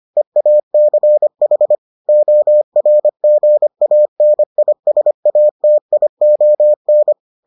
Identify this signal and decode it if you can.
Morse code, heard as a keyed tone, about 25 wpm, 600 Hz. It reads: EACH ORGANISATION